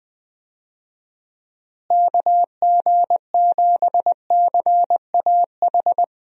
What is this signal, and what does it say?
Morse code, heard as a keyed tone, about 20 wpm, 700 Hz. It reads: KG7CAH
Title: USA Call Signs - 6 Characters - Repeated 2x - 20wpm